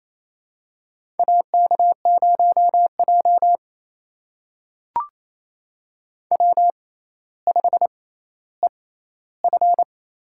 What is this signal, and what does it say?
Morse code, heard as a keyed tone, about 28 wpm, spaced as for 14 wpm, 700 Hz. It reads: AK0J W5EF
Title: USA Call Signs - 4 Characters - 6x Character Spacing - 28wpm